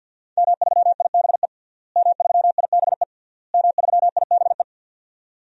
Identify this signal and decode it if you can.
Morse code, heard as a keyed tone, about 49 wpm, 700 Hz. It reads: M3IBE M3IBE M3IBE
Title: DXCC-100 Call Signs - 5 Characters - Repeated 3x - 50wpm